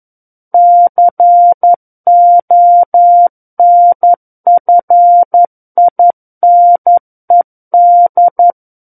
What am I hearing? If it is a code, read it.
Morse code, heard as a keyed tone, about 11 wpm, 700 Hz. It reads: CONFINED
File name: 11 WPM